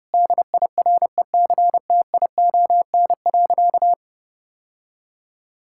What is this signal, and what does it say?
Morse code, heard as a keyed tone, about 30 wpm, 700 Hz. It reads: DIRECTION.
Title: Sentences from Top 1000 Words - Review Part 5 - 30wpm